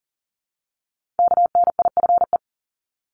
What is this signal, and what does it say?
Morse code, heard as a keyed tone, about 37 wpm, 700 Hz. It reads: KNIFE
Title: Multi-letter Phonemes Repeated 3x 40wpm